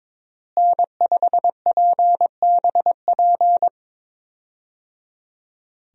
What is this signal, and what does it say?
Morse code, heard as a keyed tone, about 22 wpm, 700 Hz. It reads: N5PBP
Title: USA Call Signs - 5 Characters - Repeated 3x - 22wpm